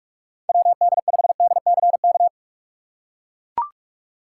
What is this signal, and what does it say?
Morse code, heard as a keyed tone, about 43 wpm, 700 Hz. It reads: WD5DCK
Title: USA Call Signs - 6 Characters - 4x Character Spacing - 45wpm